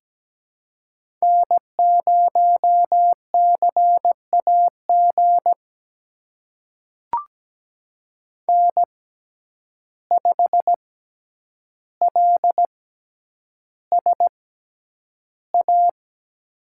Morse code, heard as a keyed tone, about 17 wpm, spaced as for 9 wpm, 700 Hz. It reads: N0CAG N5LSA
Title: USA Call Signs - 5 Characters - 6x Character Spacing - 17wpm